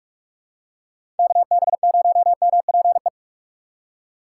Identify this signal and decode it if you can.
Morse code, heard as a keyed tone, about 46 wpm, 700 Hz. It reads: KB0MPE
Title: USA Call Signs - 6 Characters - 4x Character Spacing - 45wpm